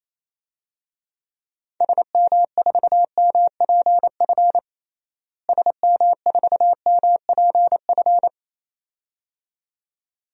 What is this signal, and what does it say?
Morse code, heard as a keyed tone, about 27 wpm, 700 Hz. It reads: SM4MPF SM4MPF
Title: DXCC-100 Call Signs - 6 Characters - Repeated 2x - 28wpm